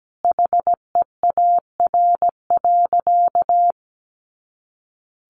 Morse code, heard as a keyed tone, about 17 wpm, 700 Hz. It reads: HEAR.
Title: Sentences from Top 600 Words 17wpm